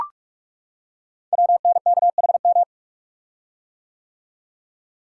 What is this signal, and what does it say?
Morse code, heard as a keyed tone, about 44 wpm, 700 Hz. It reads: WNKHM
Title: Code Groups - 5 Letters - 45wpm